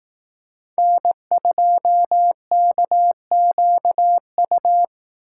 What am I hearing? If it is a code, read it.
Morse code, heard as a keyed tone, about 18 wpm, 700 Hz. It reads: N2KQU
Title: Speed Racing - USA 1x3 Call Signs 15wpm